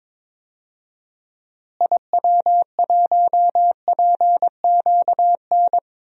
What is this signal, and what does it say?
Morse code, heard as a keyed tone, about 22 wpm, 700 Hz. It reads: IW1PQN
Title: DXCC-100 Call Signs - 6 Characters - Repeated 2x - 22wpm